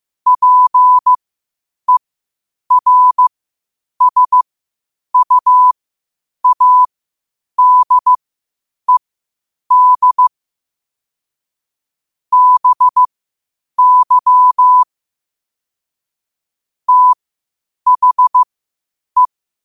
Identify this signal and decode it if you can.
Morse code, heard as a keyed tone, about 15 wpm, spaced as for 8 wpm, 1000 Hz. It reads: PERSUADED BY THE
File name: qotdfiles / 8wpm